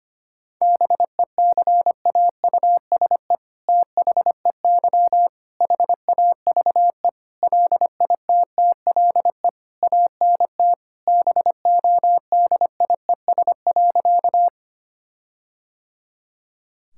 Morse code, heard as a keyed tone, about 25 wpm, 700 Hz. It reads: BECAUSE THEY HAVE LITTLE ANT BODIES.
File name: practice / 25wpm / All-Age Appropriate Jokes - 25wpm